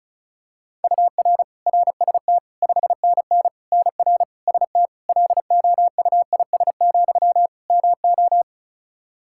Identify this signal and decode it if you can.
Morse code, heard as a keyed tone, about 36 wpm, 700 Hz. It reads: UR RST 5NN NR ST LOUIS, MO
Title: Practice QSO - Exchange 1 - Beginning - 35wpm